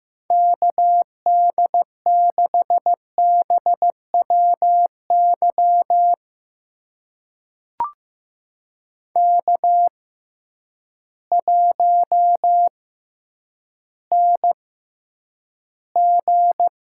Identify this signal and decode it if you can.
Morse code, heard as a keyed tone, about 15 wpm, spaced as for 9 wpm, 700 Hz. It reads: KD6BWY K1NG
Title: USA Call Signs - 6x Character Spacing - 15wpm